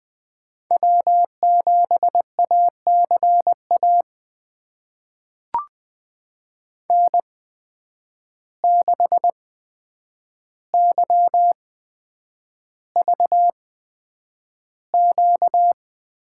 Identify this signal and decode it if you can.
Morse code, heard as a keyed tone, about 20 wpm, spaced as for 9 wpm, 700 Hz. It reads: W7ACA N6YVQ
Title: USA Call Signs - 5 Characters - 8x Character Spacing - 20wpm